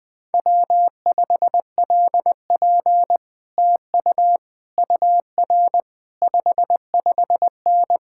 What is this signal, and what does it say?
Morse code, heard as a keyed tone, about 20 wpm, 700 Hz. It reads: W5LP TU UR 55N
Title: Speed-Racing Common QSO Phrases - 17wpm